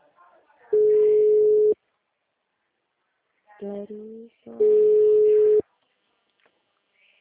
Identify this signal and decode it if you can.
Ringback tone (Europe, 425 Hz), with speech.
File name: out- -2031- - - .15383